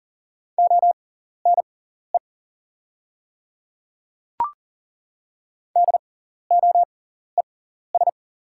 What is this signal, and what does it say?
Morse code, heard as a keyed tone, about 38 wpm, spaced as for 14 wpm, 690 Hz. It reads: ONE DOES
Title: 6x Character Spacing - Top 300 Words - 40wpm